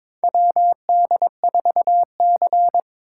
Morse code, heard as a keyed tone, about 22 wpm, 700 Hz. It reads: WD4C
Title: USA Call Signs - 4 Characters - 4x Character Spacing - 22wpm